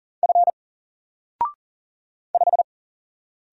Morse code, heard as a keyed tone, about 39 wpm, 700 Hz. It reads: F 5
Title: Single Character (Letter:Number:Punctuation:Prosign) 40wpm